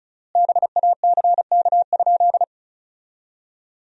Morse code, heard as a keyed tone, about 36 wpm, 700 Hz. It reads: BACK?
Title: 3x Word-Spacing - Sentences from Top 400 Words - 35wpm